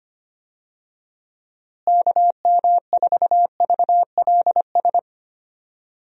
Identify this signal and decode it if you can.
Morse code, heard as a keyed tone, about 25 wpm, 700 Hz. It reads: KM4VLS
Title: USA Call Signs - 6 Characters - Repeated 3x - 25wpm